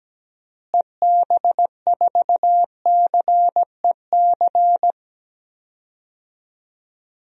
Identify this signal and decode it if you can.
Morse code, heard as a keyed tone, about 17 wpm, 700 Hz. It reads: EB4CEC
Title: DXCC-100 Call Signs - 6 Characters - Repeated 3x - 17wpm